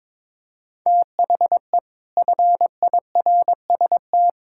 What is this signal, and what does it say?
Morse code, heard as a keyed tone, about 22 wpm, 700 Hz. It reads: THE FIRST
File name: practice / 22wpm / Sets of 7 Words - 22wpm